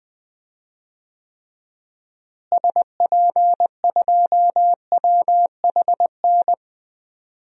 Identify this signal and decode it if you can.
Morse code, heard as a keyed tone, about 20 wpm, 700 Hz. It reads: SP2WHN
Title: DXCC-100 Call Signs - 6 Characters - Repeated 2x - 20wpm